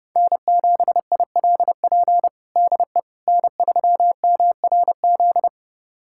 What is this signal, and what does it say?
Morse code, heard as a keyed tone, about 32 wpm, 700 Hz. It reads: N7ILP DE N3MRZ
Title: USA Callsign DE Callsign - 5 Characters - 30wpm